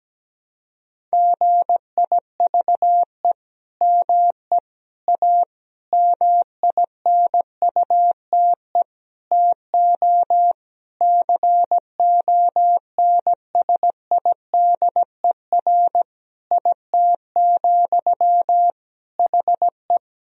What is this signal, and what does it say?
Morse code, heard as a keyed tone, about 17 wpm, 700 Hz. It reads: GIVE ME A MINUTE TO CONSIDER IT, HE
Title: Sentences from Top 1000 Words - Review Part 4 - 17wpm